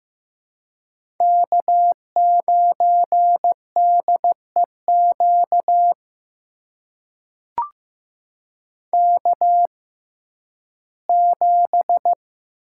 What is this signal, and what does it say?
Morse code, heard as a keyed tone, about 15 wpm, spaced as for 11 wpm, 700 Hz. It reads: K9DEQ K7
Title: USA Call Signs - 5 Characters - 6x Character Spacing - 15wpm